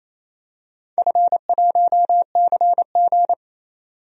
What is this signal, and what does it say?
Morse code, heard as a keyed tone, about 28 wpm, 700 Hz. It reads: F1CG
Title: DXCC-100 Call Signs - 4 Characters - 28wpm